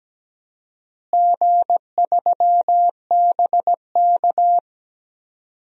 Morse code, heard as a keyed tone, about 17 wpm, 700 Hz. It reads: G3BK
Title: DXCC-100 Call Signs - 4 Characters - Repeated 3x - 17wpm